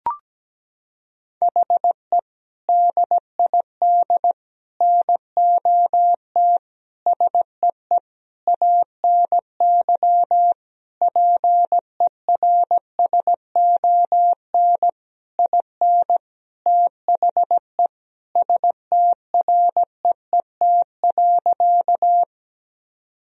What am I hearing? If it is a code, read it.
Morse code, heard as a keyed tone, about 17 wpm, 700 Hz. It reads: HE DID NOT SEE ANY PERSON IN THE STREET.
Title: Sentences from Top 1000 Words - Review Part 3 - 17wpm